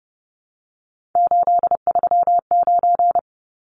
Morse code, heard as a keyed tone, about 30 wpm, 700 Hz. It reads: 839
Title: Numbers - 3-Digit 30wpm